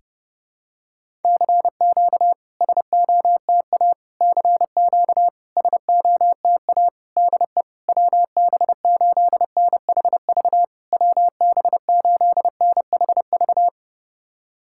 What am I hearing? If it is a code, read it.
Morse code, heard as a keyed tone, about 30 wpm, 690 Hz. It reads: CQ SOTA CQ SOTA DE WB8NHV WB8NHV
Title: SOTA QSO Practice 30wpm